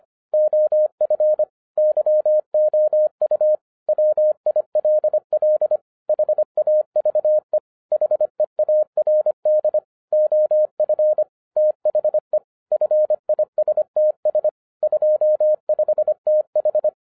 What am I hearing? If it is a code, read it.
Morse code, heard as a keyed tone, about 25 wpm, 610 Hz. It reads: OF YOU WILL HAVE HEARD OF THE FISTS 25TH
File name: 2012-01-mp3 / n14-FUN WITH NUMBERS-25wpm